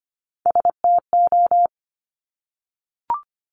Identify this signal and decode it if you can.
Morse code, heard as a keyed tone, about 25 wpm, 700 Hz. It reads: STO
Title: Top Three Letter Combinations in Words 25wpm